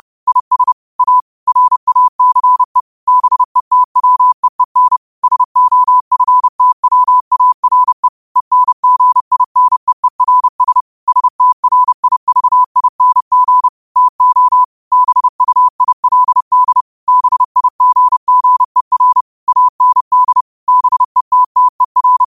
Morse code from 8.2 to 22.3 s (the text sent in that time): ENGINEERS STRIVING TO BUILD BIGGER AND BETTER